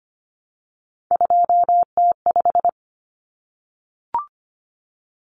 Morse code, heard as a keyed tone, about 25 wpm, 700 Hz. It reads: 2T5
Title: Common Cut Numbers - 3-Digit 25wpm